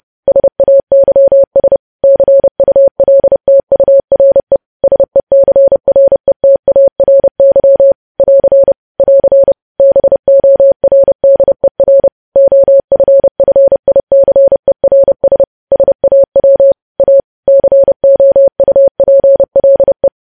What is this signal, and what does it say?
Morse code, heard as a keyed tone, about 30 wpm, 550 Hz. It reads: SAYS CULTURE SECRETARY <AR> <AR> BORDER OFFICERS SAW A COUPLE
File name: podcast / News headlines-1x-30x30